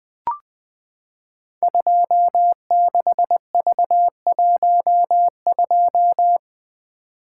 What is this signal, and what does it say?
Morse code, heard as a keyed tone, about 21 wpm, 700 Hz. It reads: 26V12
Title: Code Groups - 5 Letter:Numbers - 20wpm